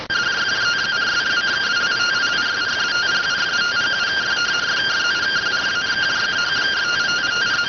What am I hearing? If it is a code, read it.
Two-tone teleprinter signal, about 170 Hz shift.